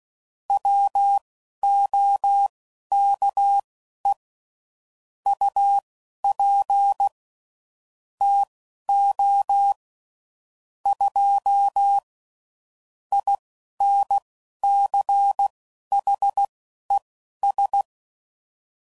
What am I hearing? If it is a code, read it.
Morse code, heard as a keyed tone, about 16 wpm, spaced as for 11 wpm, 800 Hz. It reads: WOKE UP TO 2 INCHES
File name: CW Code Groups @10WPM / Weather Sets / Weather 3 17